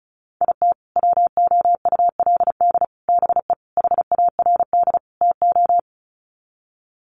Morse code, heard as a keyed tone, about 34 wpm, 700 Hz. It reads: IT WOULD BE HARD TO
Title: Sets of 5 Words Encore2 35wpm